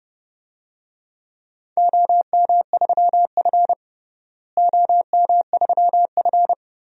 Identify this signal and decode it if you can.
Morse code, heard as a keyed tone, about 32 wpm, 700 Hz. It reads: OM3F OM3F
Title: DXCC-100 Call Signs - 4 Characters - Repeated 2x - 30wpm